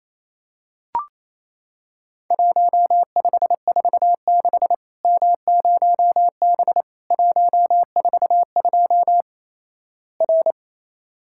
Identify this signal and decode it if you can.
Morse code, heard as a keyed tone, about 28 wpm, 700 Hz. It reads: 1546 M0B 142 R
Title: Worked All Europe DX Contest - QTC Exchange - 28wpm